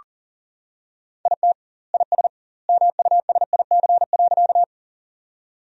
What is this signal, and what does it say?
Morse code, heard as a keyed tone, about 40 wpm, 700 Hz. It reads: IT IS MUSIC.
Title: 2x Word-Spacing - Sentences from Top 300 Words - 40wpm